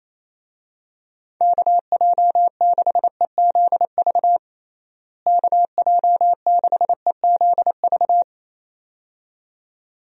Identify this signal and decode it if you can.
Morse code, heard as a keyed tone, about 28 wpm, 700 Hz. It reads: KJ6EZV KJ6EZV
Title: USA Call Signs - 6 Characters - Repeated 2x - 28wpm